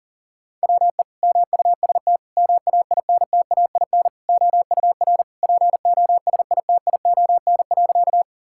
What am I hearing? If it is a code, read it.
Morse code, heard as a keyed tone, about 37 wpm, 700 Hz. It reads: WE MUST MAINTAIN OUR POSITION.
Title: Sentences from Top 1000 Words - Review Part 5 - 40wpm